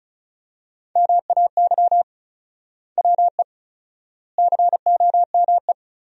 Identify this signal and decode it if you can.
Morse code, heard as a keyed tone, about 34 wpm, 700 Hz. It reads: MAY WE COME
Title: 4x Word-Spacing - Sentences from Top 100 Words - 35wpm